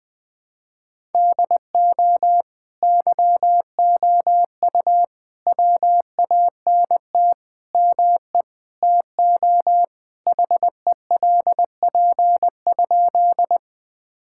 Morse code, heard as a keyed tone, about 20 wpm, 700 Hz. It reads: DO YOU WANT ME TO HELP?
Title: Sentences with 4-Word Phrases - 20wpm